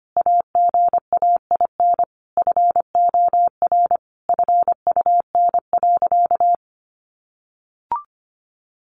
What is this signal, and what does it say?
Morse code, heard as a keyed tone, about 25 wpm, 700 Hz. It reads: AGAIN FOR FUN.
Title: Easy Phrases 25wpm